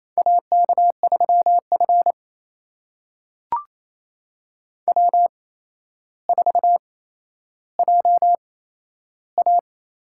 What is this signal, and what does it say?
Morse code, heard as a keyed tone, about 28 wpm, spaced as for 12 wpm, 700 Hz. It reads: AK3F W4JA
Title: USA Call Signs - 4 Characters - 8x Character Spacing - 28wpm